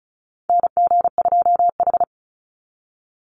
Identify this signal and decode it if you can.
Morse code, heard as a keyed tone, about 36 wpm, 700 Hz. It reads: NG2H
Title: Call Signs - 35wpm